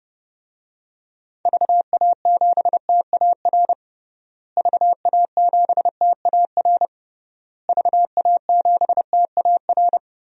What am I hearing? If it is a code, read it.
Morse code, heard as a keyed tone, about 32 wpm, 700 Hz. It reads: VA7TAR VA7TAR VA7TAR
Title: DXCC-100 Call Signs - 6 Characters - Repeated 3x - 30wpm